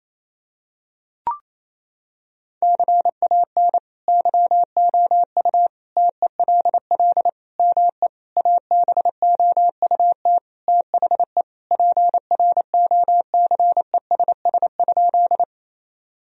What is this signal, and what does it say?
Morse code, heard as a keyed tone, about 28 wpm, 700 Hz. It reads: CAN YOU TELL ME ABOUT THE PROCESS?
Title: Sentences from Top 1000 Words - Review Part 3 - 28wpm